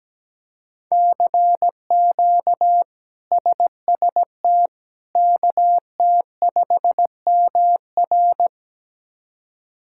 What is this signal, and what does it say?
Morse code, heard as a keyed tone, about 17 wpm, 700 Hz. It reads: CQ SST KT5MR
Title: K1USN SST QSO Practice - 17wpm